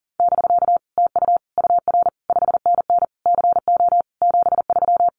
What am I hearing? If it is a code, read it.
Morse code, heard as a keyed tone, about 41 wpm, 700 Hz. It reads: <BK> TU UR 5NN CO 73
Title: Common QSO Phrases Encore 40wpm